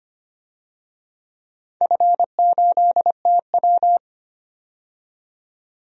Morse code, heard as a keyed tone, about 25 wpm, 700 Hz. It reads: F8TW
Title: DXCC-100 Call Signs - 4 Characters - 25wpm